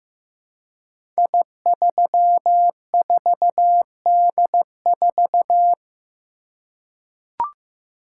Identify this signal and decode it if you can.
Morse code, heard as a keyed tone, about 15 wpm, 700 Hz. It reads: I34D4
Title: Code Groups - 5 Letter:Numbers - 15wpm